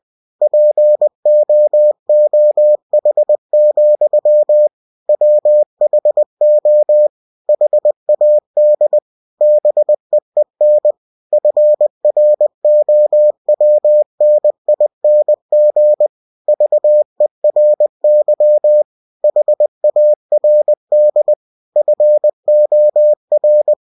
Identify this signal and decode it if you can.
Morse code, heard as a keyed tone, about 20 wpm, 600 Hz. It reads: POOH, WHO HAD BEEN FROWNING VERY HARD FOR
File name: cw-books / winnie-the-pooh / winnie-20wpm-page-0060